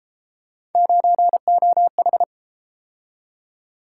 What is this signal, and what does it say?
Morse code, heard as a keyed tone, about 33 wpm, 700 Hz. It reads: 9OH